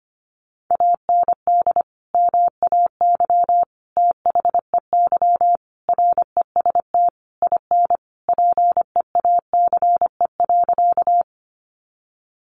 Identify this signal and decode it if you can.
Morse code, heard as a keyed tone, about 25 wpm, 700 Hz. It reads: AND MAY THEY REST IN PEACE.
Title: Sentences from Top 1500 Words 25wpm